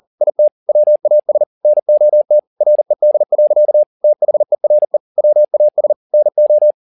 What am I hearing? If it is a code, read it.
Morse code, heard as a keyed tone, about 40 wpm, 600 Hz. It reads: IT WAS NOT RED. THERE WAS NO
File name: the old man and the sea 2 40wpm